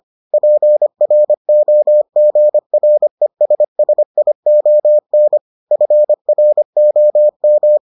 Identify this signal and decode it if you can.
Morse code, heard as a keyed tone, about 25 wpm, 600 Hz. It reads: PROGRESSION FROM